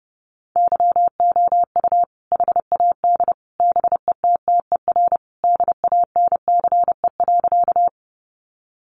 Morse code, heard as a keyed tone, about 30 wpm, 700 Hz. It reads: YOU HAD BETTER DANCE.
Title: Sentences from Top 700 Words 30wpm